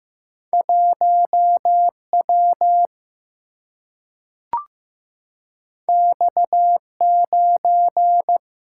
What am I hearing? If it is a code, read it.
Morse code, heard as a keyed tone, about 15 wpm, 700 Hz. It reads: 1W X9
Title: Single-Letter Single-Number 15wpm